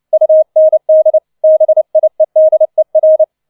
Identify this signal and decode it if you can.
Morse code, heard as a keyed tone, about 29 wpm, 630 Hz. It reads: UND BIEDER